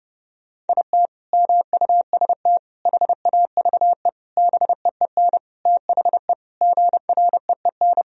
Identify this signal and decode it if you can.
Morse code, heard as a keyed tone, about 30 wpm, 700 Hz. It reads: IT MUST HAVE BEEN THE GREEN
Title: Sentences from Top 1000 Words - Review Part 2 - 30wpm